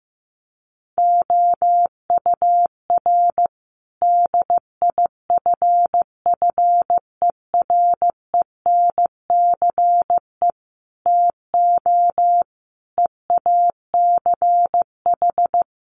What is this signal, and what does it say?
Morse code, heard as a keyed tone, about 15 wpm, 700 Hz. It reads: OUR DIFFERENCE TO EACH
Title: Sentences from Top 1500 Words 15wpm